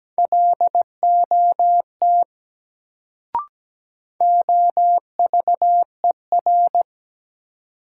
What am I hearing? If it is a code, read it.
Morse code, heard as a keyed tone, about 17 wpm, 700 Hz. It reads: LOT OVER
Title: Sets of 2 Words - New Word Drill - 17wpm